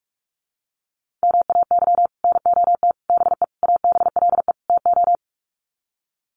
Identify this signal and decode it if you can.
Morse code, heard as a keyed tone, about 44 wpm, 700 Hz. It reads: MAY NOT BE ABLE TO
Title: Sets of 5 Words 45wpm